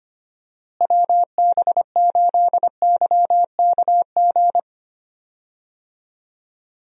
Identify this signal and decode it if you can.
Morse code, heard as a keyed tone, about 25 wpm, 700 Hz. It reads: WB8YKG
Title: USA Call Signs - 6 Characters - Repeated 2x - 25wpm